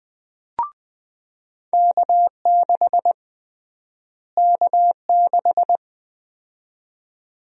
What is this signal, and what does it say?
Morse code, heard as a keyed tone, about 20 wpm, 700 Hz. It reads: K6 K6
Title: USA Call Sign Prefixes - Repeated 2x - 20wpm